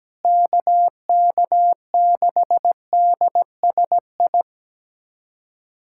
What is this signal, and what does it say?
Morse code, heard as a keyed tone, about 17 wpm, 700 Hz. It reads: KK6DSI
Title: USA Callsign DE Callsign - 6 Characters - 17wpm